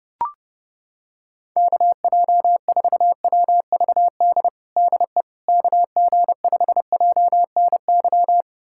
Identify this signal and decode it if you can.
Morse code, heard as a keyed tone, about 30 wpm, 700 Hz. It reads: KJ4WVD DE KG5JNY
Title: USA Callsign DE Callsign - 6 Characters - 30wpm